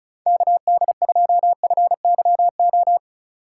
Morse code, heard as a keyed tone, about 36 wpm, 700 Hz. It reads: KD2FYO
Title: USA Call Signs - 6 Characters - 4x Character Spacing - 35wpm